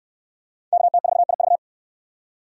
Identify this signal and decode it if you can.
Morse code, heard as a keyed tone, about 65 wpm, 700 Hz. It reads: KT4IV